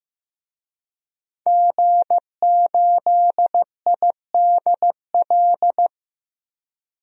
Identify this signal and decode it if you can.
Morse code, heard as a keyed tone, about 15 wpm, 700 Hz. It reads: G8IDL
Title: DXCC-100 Call Signs - 5 Characters - 15wpm